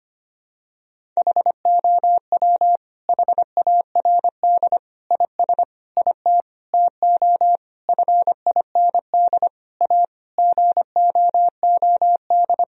Morse code, heard as a keyed tone, about 25 wpm, 700 Hz. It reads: HOW HARD IS IT TO FIND A GOOD
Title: Sentences from Top 1000 Words - Review Part 1 - 25wpm